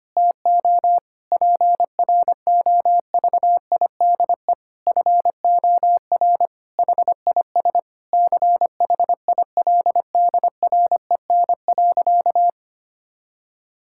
Morse code, heard as a keyed tone, about 25 wpm, 700 Hz. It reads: TO PROVIDE FOR HIS CHILDREN.
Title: Sentences from Top 1600 Words 25wpm